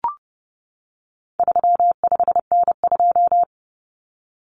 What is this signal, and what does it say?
Morse code, heard as a keyed tone, about 30 wpm, 700 Hz. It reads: 35N2
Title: Common Cut Numbers - 4-Digit 30wpm